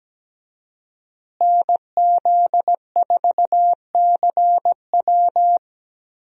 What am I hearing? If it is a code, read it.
Morse code, heard as a keyed tone, about 17 wpm, 700 Hz. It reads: NZ4CW
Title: USA Call Signs - 5 Characters - Repeated 2x - 17wpm